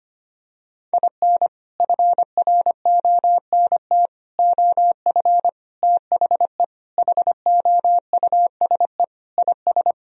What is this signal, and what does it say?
Morse code, heard as a keyed tone, about 25 wpm, 700 Hz. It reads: IN FRONT OF THE HOUSE IS
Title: Sentences from Top 1000 Words - Review Part 4 - 25wpm